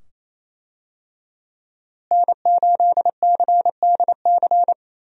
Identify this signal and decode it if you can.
Morse code, heard as a keyed tone, about 28 wpm, 700 Hz. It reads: N8CDC
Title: USA Call Signs - 5 Characters - Repeated 2x - 28wpm